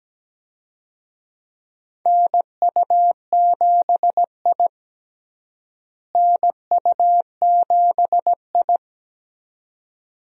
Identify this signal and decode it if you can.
Morse code, heard as a keyed tone, about 17 wpm, 700 Hz. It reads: NU7I NU7I
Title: USA Call Signs - 4 Characters - Repeated 2x - 17wpm